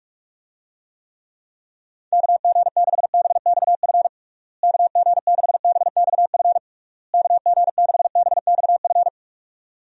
Morse code, heard as a keyed tone, about 46 wpm, 700 Hz. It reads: KG6BXF KG6BXF KG6BXF
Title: USA Call Signs - 6 Characters - Repeated 3x - 45wpm